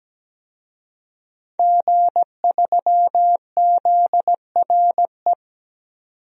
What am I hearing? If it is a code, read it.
Morse code, heard as a keyed tone, about 17 wpm, 700 Hz. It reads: G3ZRE
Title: DXCC-100 Call Signs - 5 Characters - 17wpm